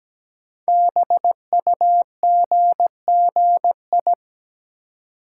Morse code, heard as a keyed tone, about 17 wpm, 700 Hz. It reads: BUGGI
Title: Code Groups - 5 Letter:Numbers - 17wpm